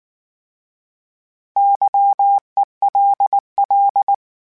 Morse code, heard as a keyed tone, about 19 wpm, 800 Hz. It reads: YELL